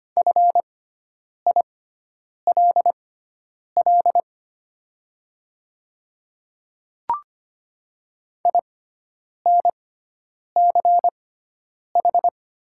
Morse code, heard as a keyed tone, about 25 wpm, spaced as for 9 wpm, 700 Hz. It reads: FILL INCH
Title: 6x Character Spacing - Top 500 Words - 25wpm